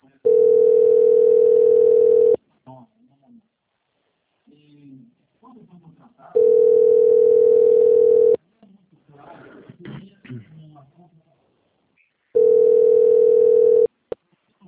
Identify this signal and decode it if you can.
Ringback tone (North America), 440 + 480 Hz, with speech.
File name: out- - - - - .9680